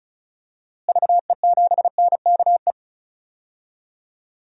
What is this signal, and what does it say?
Morse code, heard as a keyed tone, about 36 wpm, 700 Hz. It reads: VE7NKE
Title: DXCC-100 Call Signs - 6 Characters - Repeated 3x - 35wpm